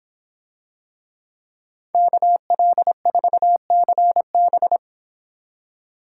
Morse code, heard as a keyed tone, about 26 wpm, 700 Hz. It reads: KL4CB